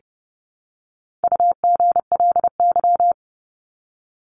Morse code, heard as a keyed tone, about 30 wpm, 700 Hz. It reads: UGLY
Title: Top 1751-2000 Words 30wpm